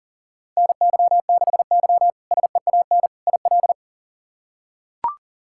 Morse code, heard as a keyed tone, about 38 wpm, 700 Hz. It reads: NY6Y SEAN IL
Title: K1USN SST QSO Practice - 40wpm